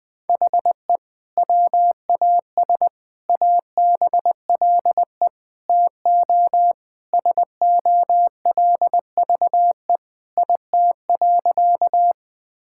Morse code, heard as a keyed tone, about 21 wpm, 700 Hz. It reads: HE WAS ABLE TO SOLVE IT.
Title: Sentences with 3-Word Phrases - Spoken Beginning - 20wpm